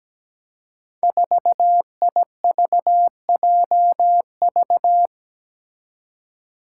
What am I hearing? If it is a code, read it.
Morse code, heard as a keyed tone, about 17 wpm, 700 Hz. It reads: 4IVJV
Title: Five Challenging Letter-Number Combinations 17wpm